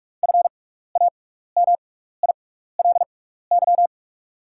Morse code, heard as a keyed tone, about 44 wpm, spaced as for 15 wpm, 690 Hz. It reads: FAMILY
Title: 6x Character Spacing - Top 500 Words - 45wpm